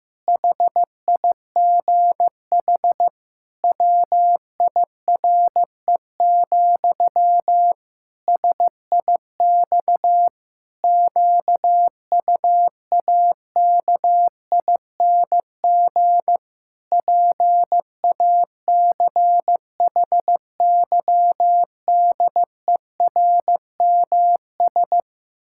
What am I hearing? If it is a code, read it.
Morse code, heard as a keyed tone, about 16 wpm, 700 Hz. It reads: HIGH WIRE, SIX QUAKING PACHYDERMS
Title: Pangram Challenge v2 - 15wpm